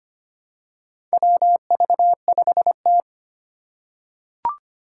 Morse code, heard as a keyed tone, about 25 wpm, 700 Hz. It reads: WV5T
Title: Four Challenging Letter-Number Combinations 25wpm